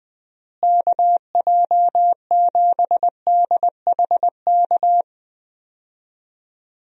Morse code, heard as a keyed tone, about 20 wpm, 700 Hz. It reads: KJ7DHK
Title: USA Call Signs - 6 Characters - Repeated 2x - 20wpm